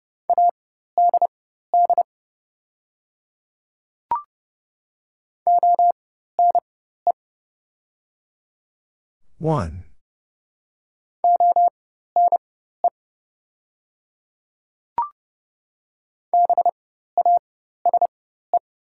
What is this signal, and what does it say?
Morse code, heard as a keyed tone, about 30 wpm, spaced as for 14 wpm, 700 Hz. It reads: ADD ONE ONE BASE
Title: 4x Character Spacing - Top 500 Words - 30wpm